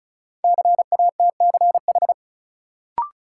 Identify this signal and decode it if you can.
Morse code, heard as a keyed tone, about 35 wpm, 700 Hz. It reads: CATCH
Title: Sentences from Top 1500 Words - New Word Drill - 35wpm